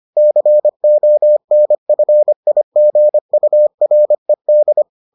Morse code, heard as a keyed tone, about 25 wpm, 600 Hz. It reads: CONFIGURED